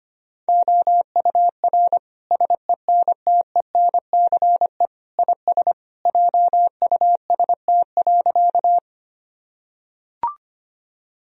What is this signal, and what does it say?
Morse code, heard as a keyed tone, about 25 wpm, 700 Hz. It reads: OUR SENTENCE IS JUST.
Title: Sentences from Top 1000 Words - Review Part 1 - 25wpm